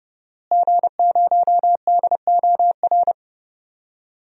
Morse code, heard as a keyed tone, about 30 wpm, 700 Hz. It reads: G0DOR
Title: DXCC-100 Call Signs - 5 Characters - Repeated 2x - 30wpm